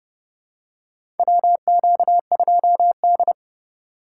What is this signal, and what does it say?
Morse code, heard as a keyed tone, about 32 wpm, 700 Hz. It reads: WQ2D
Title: USA Call Signs - 4 Characters - 3x Character Spacing - 30wpm